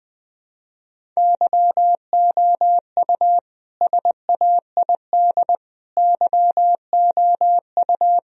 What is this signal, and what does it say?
Morse code, heard as a keyed tone, about 21 wpm, 700 Hz. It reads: YOU SAID YOU
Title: Sentences from Top 1000 Words - Review Part 1 - 20wpm